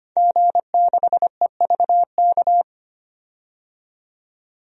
Morse code, heard as a keyed tone, about 25 wpm, 700 Hz. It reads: G6EVK
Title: DXCC-100 Call Signs - 5 Characters - 25wpm